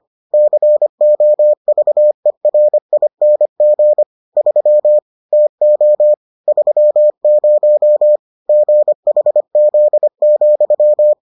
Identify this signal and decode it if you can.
Morse code, heard as a keyed tone, about 25 wpm, 600 Hz. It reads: COVERING 3 TO 30 GHZ,